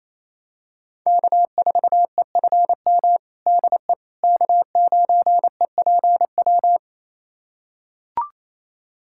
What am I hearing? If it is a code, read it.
Morse code, heard as a keyed tone, about 27 wpm, 700 Hz. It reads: K4EFM DE K9EPW
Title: USA Callsign DE Callsign - 5 Characters - 28wpm